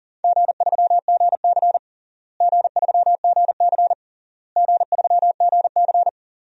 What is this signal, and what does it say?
Morse code, heard as a keyed tone, about 40 wpm, 700 Hz. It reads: G3GC G3GC G3GC
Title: DXCC-100 Call Signs - 4 Characters - Repeated 3x - 40wpm